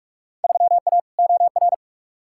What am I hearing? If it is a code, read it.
Morse code, heard as a keyed tone, about 47 wpm, 700 Hz. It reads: 3A OR
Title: Field Day Exchanges - 45wpm